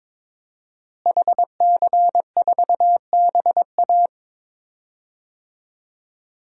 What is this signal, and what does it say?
Morse code, heard as a keyed tone, about 22 wpm, 700 Hz. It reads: HC4BA
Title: DXCC-100 Call Signs - 5 Characters - Repeated 3x - 22wpm